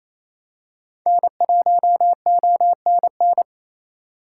Morse code, heard as a keyed tone, about 28 wpm, 700 Hz. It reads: N1ONN
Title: USA Call Signs - 5 Characters - 4x Character Spacing - 28wpm